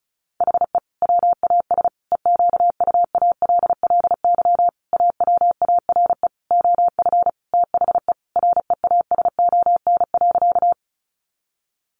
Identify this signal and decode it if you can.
Morse code, heard as a keyed tone, about 34 wpm, 700 Hz. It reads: HE WAS EQUALLY AWARE OF THE REASON.
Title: Sentences from Top 1400 Words 35wpm